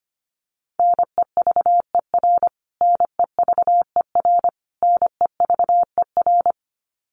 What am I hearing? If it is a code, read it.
Morse code, heard as a keyed tone, about 25 wpm, 700 Hz. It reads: NEVER NEVER NEVER
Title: Top 300 Words Repeated 25wpm